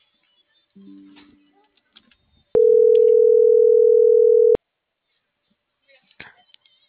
Ringback tone (North America), 440 + 480 Hz.